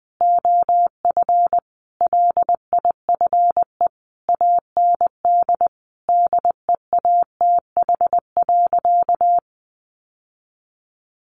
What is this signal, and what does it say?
Morse code, heard as a keyed tone, about 20 wpm, 700 Hz. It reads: OF LIFE AND DEATH.
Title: Sentences from Top 1100 Words 20wpm